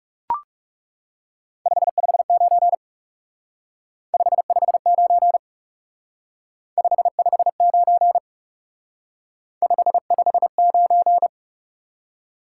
Morse code, heard as a keyed tone, about 37 wpm, 700 Hz. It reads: 559 559 559 559
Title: Speed-Racing Top 90 QSO Elements 30wpm